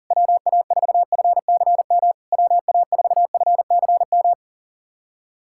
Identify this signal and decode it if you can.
Morse code, heard as a keyed tone, about 40 wpm, 700 Hz. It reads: WA4FCM WA4FCM
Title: SOTA QSO Practice 40wpm